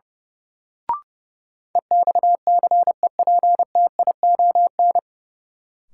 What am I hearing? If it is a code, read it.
Morse code, heard as a keyed tone, about 30 wpm, 700 Hz. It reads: EXCEPTION
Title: Sets of 4 Words - New Word Drill - 30wpm